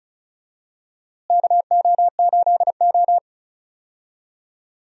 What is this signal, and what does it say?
Morse code, heard as a keyed tone, about 36 wpm, 700 Hz. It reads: KO8O
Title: USA Call Signs - 4 Characters - Repeated 2x - 35wpm